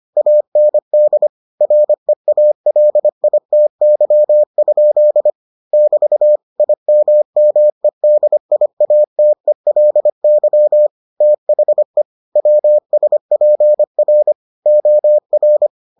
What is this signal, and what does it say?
Morse code, heard as a keyed tone, about 25 wpm, 600 Hz. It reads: AND REALITY? <BT> IMMEDIATELY THE WSPR OR